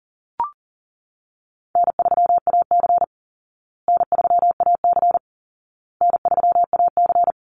Morse code, heard as a keyed tone, about 40 wpm, 700 Hz. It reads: N3AC N3AC N3AC
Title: Call Signs Repeated 2x - 40wpm